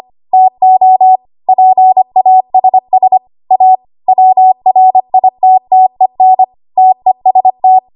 Morse code, heard as a keyed tone, about 25 wpm, 750 Hz. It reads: TO PASS A WRITTEN TEST
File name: Archive / 25 WPM / 25WPM